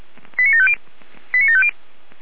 Five-tone selective call (ZVEI), heard as quick runs of five tones, tones of about 70 ms. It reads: 89750 89750
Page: ZVEI